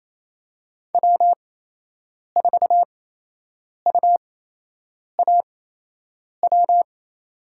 Morse code, heard as a keyed tone, about 28 wpm, spaced as for 8 wpm, 700 Hz. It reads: W4UAW
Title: USA Call Signs - 8x Character Spacing - 28wpm